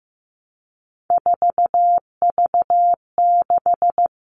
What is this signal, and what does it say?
Morse code, heard as a keyed tone, about 15 wpm, 700 Hz. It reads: 4V6
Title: All Cut Numbers - 3-Digit 15wpm